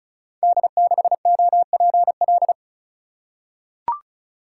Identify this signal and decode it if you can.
Morse code, heard as a keyed tone, about 36 wpm, 700 Hz. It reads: D6OPL
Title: Code Groups - 5 Letter:Numbers - 35wpm